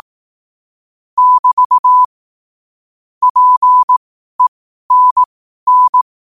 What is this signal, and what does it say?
Morse code, heard as a keyed tone, about 18 wpm, spaced as for 12 wpm, 1000 Hz. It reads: <BT> PENN
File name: qotdfiles / 12wpm